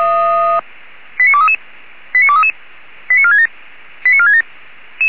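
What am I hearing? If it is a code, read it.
Five-tone selective call (ZVEI), heard as quick runs of five tones, tones of about 70 ms. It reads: 89233 89233 89467 89467
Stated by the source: Five-tone ZVEI selective call received on an SDR, address 89243, channel 464.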